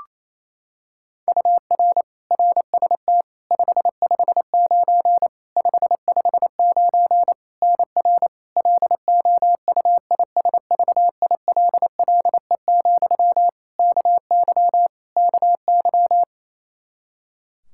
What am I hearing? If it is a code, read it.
Morse code, heard as a keyed tone, about 28 wpm, 700 Hz. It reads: UR RST 559 559 NR LOUISVILLE, KY KY
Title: Practice QSO - Exchange 1 - Beginning - 28wpm